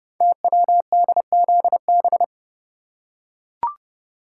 Morse code, heard as a keyed tone, about 28 wpm, 700 Hz. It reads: TWDZB
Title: Code Groups - 5 Letters - 30wpm